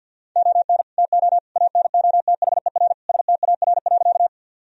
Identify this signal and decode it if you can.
Morse code, heard as a keyed tone, about 50 wpm, 700 Hz. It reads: ON TO ANOTHER STAR.
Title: Sentences from Top 1000 Words - Review Part 2 - 50wpm